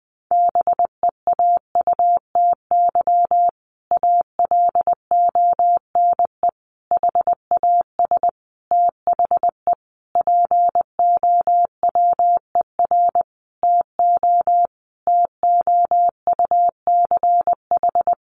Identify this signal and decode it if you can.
Morse code, heard as a keyed tone, about 20 wpm, 700 Hz. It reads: BEAUTY ALONE HAS THE POWER TO TOUCH
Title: Sentences from Top 1200 Words 20wpm